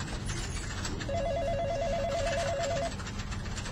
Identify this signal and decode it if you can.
There is no code.